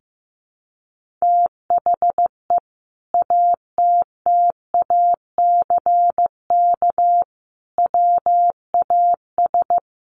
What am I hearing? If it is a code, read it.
Morse code, heard as a keyed tone, about 14 wpm, 700 Hz. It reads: THE ATTACK WAS
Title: Sentences from Top 1500 Words 15wpm